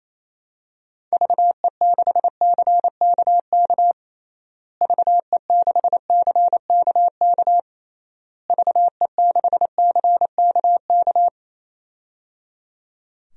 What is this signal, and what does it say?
Morse code, heard as a keyed tone, about 28 wpm, 700 Hz. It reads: VE6CKK VE6CKK VE6CKK
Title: DXCC-100 Call Signs - 6 Characters - Repeated 3x - 28wpm